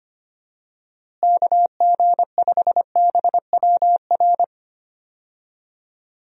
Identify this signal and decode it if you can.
Morse code, heard as a keyed tone, about 25 wpm, 700 Hz. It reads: KG5BWR
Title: USA Call Signs - 6 Characters - Repeated 3x - 25wpm